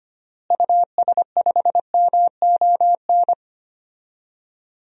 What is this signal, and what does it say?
Morse code, heard as a keyed tone, about 25 wpm, 700 Hz. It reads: US5MON
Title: DXCC-100 Call Signs - 6 Characters - Repeated 3x - 25wpm